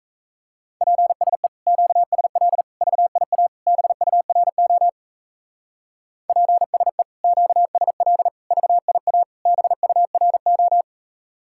Morse code, heard as a keyed tone, about 36 wpm, 700 Hz. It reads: PSE QSL VIA BURO PSE QSL VIA BURO